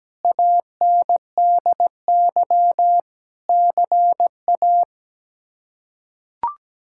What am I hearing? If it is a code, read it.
Morse code, heard as a keyed tone, about 17 wpm, 700 Hz. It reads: ANDY CA
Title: K1USN SST QSO Practice - 17wpm